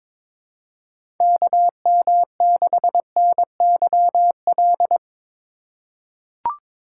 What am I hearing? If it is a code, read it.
Morse code, heard as a keyed tone, about 22 wpm, 700 Hz. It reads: KM6NYL
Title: USA Call Signs - 6 Characters - 3x Character Spacing - 22wpm